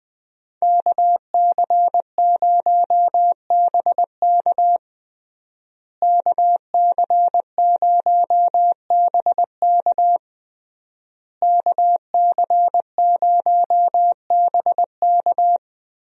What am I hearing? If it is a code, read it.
Morse code, heard as a keyed tone, about 20 wpm, 700 Hz. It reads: KC0BK KC0BK KC0BK
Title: USA Call Signs - 5 Characters - Repeated 3x - 20wpm